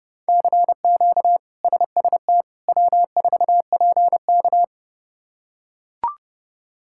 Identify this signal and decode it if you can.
Morse code, heard as a keyed tone, about 30 wpm, 700 Hz. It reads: CQ SST W4PK
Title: K1USN SST QSO Practice - 30wpm